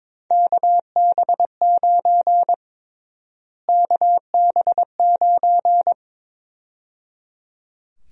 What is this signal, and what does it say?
Morse code, heard as a keyed tone, about 22 wpm, 700 Hz. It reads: KB9 KB9
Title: USA Call Sign Prefixes - Repeated 2x - 22wpm